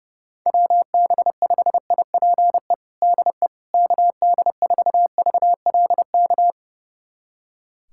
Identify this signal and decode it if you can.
Morse code, heard as a keyed tone, about 30 wpm, 700 Hz. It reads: WB5IPE DE KD4VLK
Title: USA Callsign DE Callsign 30wpm